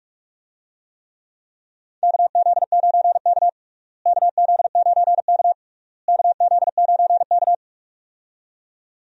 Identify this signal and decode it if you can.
Morse code, heard as a keyed tone, about 42 wpm, 700 Hz. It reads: KZ9K KZ9K KZ9K
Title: USA Call Signs - 4 Characters - Repeated 3x - 45wpm